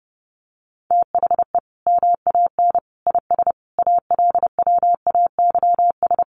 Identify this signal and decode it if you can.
Morse code, heard as a keyed tone, about 30 wpm, 700 Hz. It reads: THE MAN IS ALWAYS
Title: Sentences from Top 1400 Words 30wpm